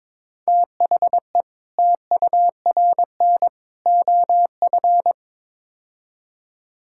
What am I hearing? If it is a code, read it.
Morse code, heard as a keyed tone, about 22 wpm, 700 Hz. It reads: THE TURN OF
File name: practice / 22wpm / Build A Phrase - 5-Word Phrases - 22wpm